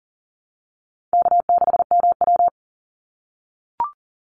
Morse code, heard as a keyed tone, about 39 wpm, 700 Hz. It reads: K6MW
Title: NA SOTA Chaser Callsigns 40wpm